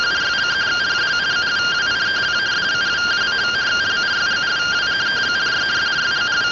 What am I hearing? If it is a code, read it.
Two-tone teleprinter signal, about 170 Hz shift.